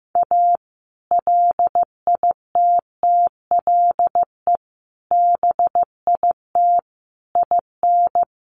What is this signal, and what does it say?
Morse code, heard as a keyed tone, about 15 wpm, 700 Hz. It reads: A LITTLE BIT IN
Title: Sentences from Top 900 Words 15wpm